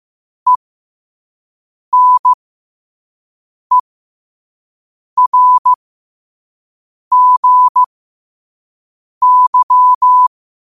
Morse code, heard as a keyed tone, about 15 wpm, spaced as for 5 wpm, 1000 Hz. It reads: ENERGY
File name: qotdfiles / 5wpm